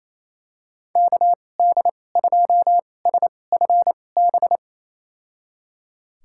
Morse code, heard as a keyed tone, about 28 wpm, spaced as for 20 wpm, 700 Hz. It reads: KD2SFB
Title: USA Call Signs - 6 Characters - 2x Character Spacing - 28wpm